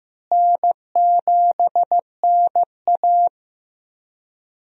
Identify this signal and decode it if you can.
Morse code, heard as a keyed tone, about 15 wpm, 700 Hz. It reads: N7NA
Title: USA Call Signs - 4 Characters - Repeated 3x - 15wpm